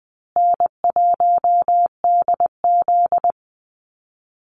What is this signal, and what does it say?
Morse code, heard as a keyed tone, about 20 wpm, 700 Hz. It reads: N1DZ
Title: USA Call Signs Repeated 2x 20wpm